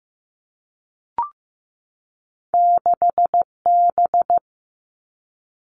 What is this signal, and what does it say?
Morse code, heard as a keyed tone, about 15 wpm, 700 Hz. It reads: 6B
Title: Two Challenging Letter-Number Combinations 15wpm